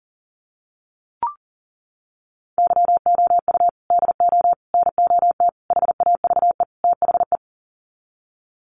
Morse code, heard as a keyed tone, about 40 wpm, 700 Hz. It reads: YOU DO NOT HAVE THE
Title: Sets of 5 Words 40wpm